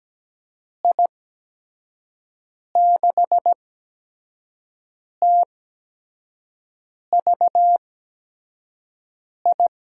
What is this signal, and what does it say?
Morse code, heard as a keyed tone, about 17 wpm, spaced as for 5 wpm, 700 Hz. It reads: I6TVI
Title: USA Call Signs - 6 Characters - 8x Character Spacing - 17wpm